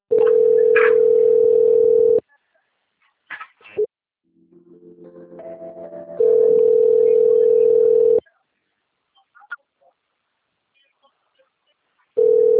Ringback tone (North America), 440 + 480 Hz.